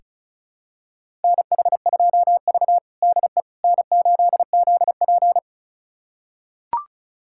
Morse code, heard as a keyed tone, about 34 wpm, 700 Hz. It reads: NH2V DE N8ZP
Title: USA Callsign DE Callsign - 4 Characters - 35wpm